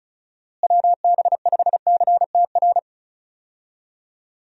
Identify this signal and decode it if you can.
Morse code, heard as a keyed tone, about 36 wpm, 700 Hz. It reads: WB5CTR
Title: USA Call Signs - 6 Characters - Repeated 2x - 35wpm